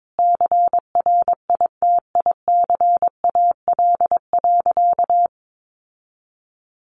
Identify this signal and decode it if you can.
Morse code, heard as a keyed tone, about 22 wpm, 700 Hz. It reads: CRITICAL.
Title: Sentences from Top 1100 Words 22wpm